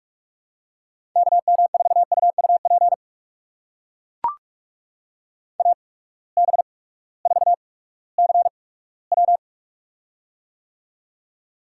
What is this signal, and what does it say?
Morse code, heard as a keyed tone, about 43 wpm, spaced as for 20 wpm, 700 Hz. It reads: KM4UUP AB4CW
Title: USA Call Signs - 8x Character Spacing - 45wpm